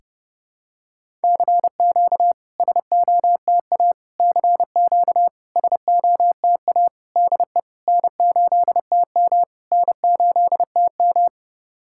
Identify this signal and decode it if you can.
Morse code, heard as a keyed tone, about 30 wpm, 700 Hz. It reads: CQ SOTA CQ SOTA DE N8TM N8TM
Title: SOTA QSO Practice 30wpm